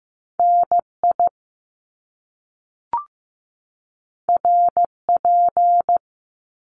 Morse code, heard as a keyed tone, about 15 wpm, 700 Hz. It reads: NI RP
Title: Top Two Letter Combinations in Words 15wpm